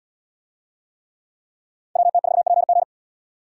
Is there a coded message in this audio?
Yes: Morse code from 2.0 to 2.8 s, heard as a keyed tone, about 71 wpm, 700 Hz. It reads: WT4PG